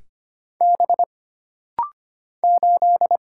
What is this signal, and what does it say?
Morse code, heard as a keyed tone, about 25 wpm, 700 Hz. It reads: B 8
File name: Single Character (Letter:Number:Punctuation:Prosign) - Rapid-Fire - 25wpm